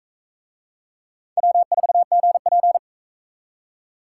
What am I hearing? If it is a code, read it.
Morse code, heard as a keyed tone, about 43 wpm, 700 Hz. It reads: W4GP